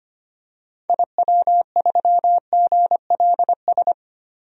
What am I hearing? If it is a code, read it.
Morse code, heard as a keyed tone, about 25 wpm, 700 Hz. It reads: IW3GLS
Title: DXCC-100 Call Signs - 6 Characters - 25wpm